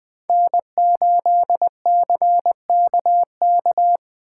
Morse code, heard as a keyed tone, about 20 wpm, 700 Hz. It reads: N8CKK
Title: Speed Racing - USA Call Signs 17wpm